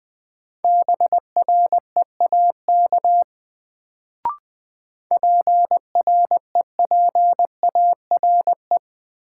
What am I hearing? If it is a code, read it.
Morse code, heard as a keyed tone, about 20 wpm, 700 Hz. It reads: BREAK PREPARE
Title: Sentences from Top 900 Words - New Word Drill - 20wpm